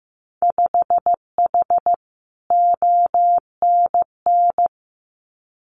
Morse code, heard as a keyed tone, about 15 wpm, 700 Hz. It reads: 5H ONN
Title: Winter Field Day Exchanges Encore 15wpm